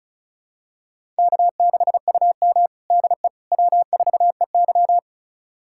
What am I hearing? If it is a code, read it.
Morse code, heard as a keyed tone, about 36 wpm, 700 Hz. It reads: K6UM DE W4EY
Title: USA Callsign DE Callsign - 4 Characters - 35wpm